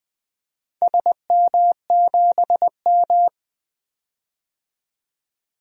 Morse code, heard as a keyed tone, about 20 wpm, 700 Hz. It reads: SM7M
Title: DXCC-100 Call Signs - 4 Characters - 20wpm